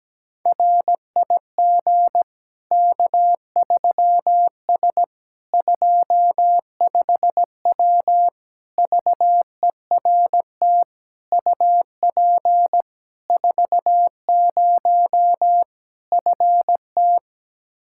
Morse code, heard as a keyed tone, about 17 wpm, 700 Hz. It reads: RIG K3S 25W VERT UP 40 FT
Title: K1USN SST QSO Practice 17wpm